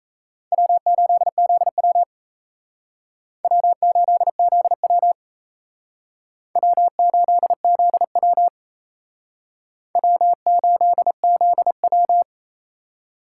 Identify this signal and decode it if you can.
Morse code, heard as a keyed tone, about 34 wpm, 700 Hz. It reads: W8ZW W8ZW W8ZW W8ZW
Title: Speed Racing - USA 1x2 Call Signs 28wpm